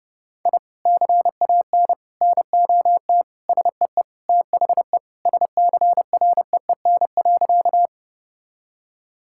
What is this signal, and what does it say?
Morse code, heard as a keyed tone, about 32 wpm, 700 Hz. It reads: I CAN NOT SEE THE SCREEN.
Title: Sentences from Top 1000 Words - Review Part 6 - 30wpm